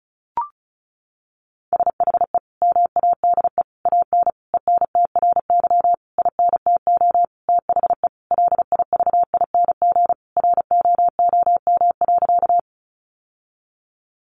Morse code, heard as a keyed tone, about 34 wpm, 700 Hz. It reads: SHE MADE AN ENTRY INTO THE LIVING ROOM.
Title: Sentences from Top 1500 Words 35wpm